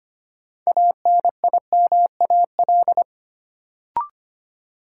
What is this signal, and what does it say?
Morse code, heard as a keyed tone, about 25 wpm, 700 Hz. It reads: ANIMAL
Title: Sentences from Top 1000 Words - Review Part 1 - New Word Drill - 25wpm